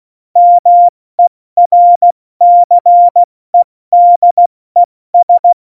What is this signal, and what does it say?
Morse code, heard as a keyed tone, about 16 wpm, 700 Hz. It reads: MERCEDES